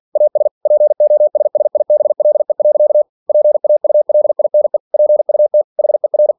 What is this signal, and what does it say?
Morse code, heard as a keyed tone, about 47 wpm, 600 Hz. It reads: AS POSSIBLE. PAULINE PUT HER
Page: Never 47wpm